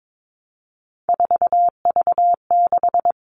Morse code, heard as a keyed tone, about 22 wpm, 700 Hz. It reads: 4V6
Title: All Cut Numbers - 3-Digit 22wpm